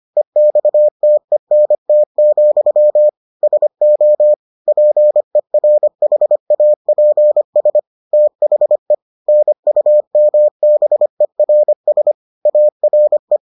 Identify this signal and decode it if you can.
Morse code, heard as a keyed tone, about 25 wpm, 600 Hz. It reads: EXTENT, SO PERHAPS THE NUMBERS ARE